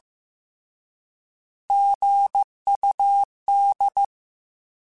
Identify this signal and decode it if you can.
Morse code, heard as a keyed tone, about 15 wpm, 800 Hz. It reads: GUD
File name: Abreviations / Abreviations mp3s / Abreviations@13